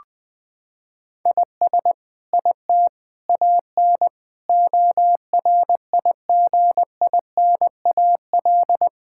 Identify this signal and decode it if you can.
Morse code, heard as a keyed tone, about 20 wpm, 700 Hz. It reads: IS IT AN ORIGINAL
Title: Sentences from Top 1000 Words - Review Part 6 - 20wpm